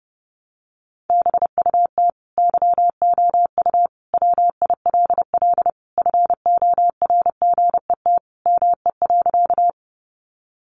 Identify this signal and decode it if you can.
Morse code, heard as a keyed tone, about 30 wpm, 700 Hz. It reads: BUT YOU WILL FORGET ME.
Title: Sentences from Top 1200 Words 30wpm